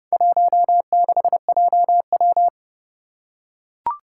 Morse code, heard as a keyed tone, about 30 wpm, 700 Hz. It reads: 16JW
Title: Four Challenging Letter-Number Combinations 30wpm